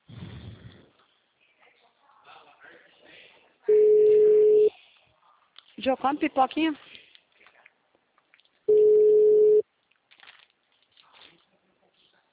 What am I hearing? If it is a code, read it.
Ringback tone (Europe, 425 Hz), with speech.